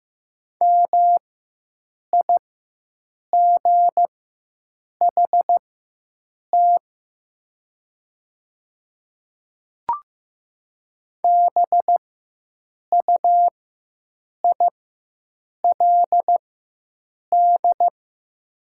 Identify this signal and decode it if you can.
Morse code, heard as a keyed tone, about 15 wpm, spaced as for 7 wpm, 700 Hz. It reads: MIGHT BUILD
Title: 4x Character Spacing - Top 300 Words - 15wpm